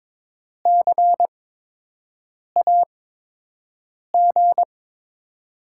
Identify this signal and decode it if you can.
Morse code, heard as a keyed tone, about 22 wpm, spaced as for 6 wpm, 700 Hz. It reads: CAG
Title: USA Call Signs - 6 Characters - 8x Character Spacing - 22wpm